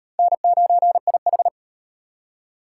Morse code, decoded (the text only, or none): N9IH